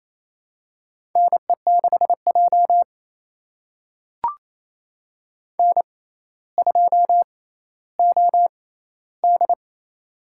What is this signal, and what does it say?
Morse code, heard as a keyed tone, about 28 wpm, spaced as for 14 wpm, 700 Hz. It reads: NE6J N2OD
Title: USA Call Signs - 4 Characters - 6x Character Spacing - 28wpm